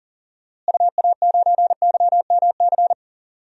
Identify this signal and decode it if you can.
Morse code, heard as a keyed tone, about 40 wpm, 700 Hz. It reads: UA9YMC
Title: DXCC-100 Call Signs - 6 Characters - Repeated 2x - 40wpm